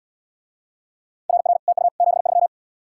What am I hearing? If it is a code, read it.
Morse code, heard as a keyed tone, about 80 wpm, 700 Hz. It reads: RR TU 73